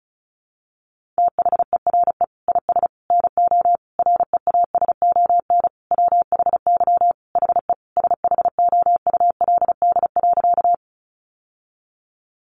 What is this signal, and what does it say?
Morse code, heard as a keyed tone, about 34 wpm, 700 Hz. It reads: THERE IS NO REASON WHY HE SHOULD.
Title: Sentences from Top 700 Words 35wpm